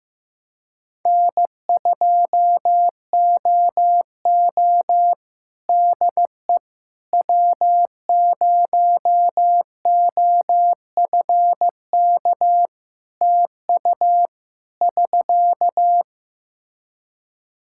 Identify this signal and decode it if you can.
Morse code, heard as a keyed tone, about 16 wpm, 700 Hz. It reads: N2OO DE W0OFK TU <SK>
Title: K1USN SST QSO Practice 15wpm